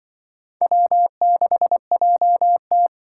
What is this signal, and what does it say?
Morse code, heard as a keyed tone, about 24 wpm, 700 Hz. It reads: W6JT
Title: Speed Racing - USA 1x2 Call Signs 20wpm